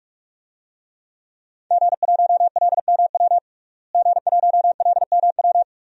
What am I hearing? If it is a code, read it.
Morse code, heard as a keyed tone, about 46 wpm, 700 Hz. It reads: G1LMW G1LMW
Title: DXCC-100 Call Signs - 5 Characters - Repeated 2x - 45wpm